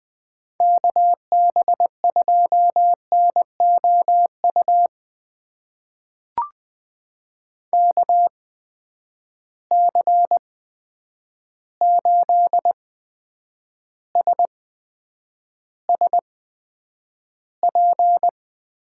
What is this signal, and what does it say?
Morse code, heard as a keyed tone, about 20 wpm, spaced as for 9 wpm, 700 Hz. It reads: KB2NOU KC8SSP
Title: USA Call Signs - 6 Characters - 8x Character Spacing - 20wpm